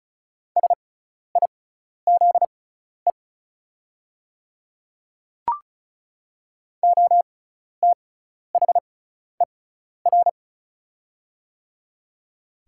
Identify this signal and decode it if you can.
Morse code, heard as a keyed tone, about 36 wpm, spaced as for 12 wpm, 700 Hz. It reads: SIZE OTHER
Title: 6x Character Spacing - Top 500 Words - 35wpm